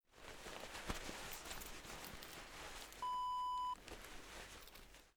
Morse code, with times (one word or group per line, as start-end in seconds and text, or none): none